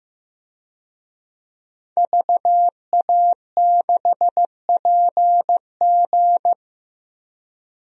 Morse code, heard as a keyed tone, about 15 wpm, 700 Hz. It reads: VA6PG
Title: DXCC-100 Call Signs - 5 Characters - Repeated 2x - 15wpm